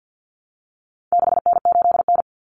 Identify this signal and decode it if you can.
Morse code, heard as a keyed tone, about 49 wpm, 700 Hz. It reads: 6N8N
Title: Common Cut Numbers - 4-Digit 50wpm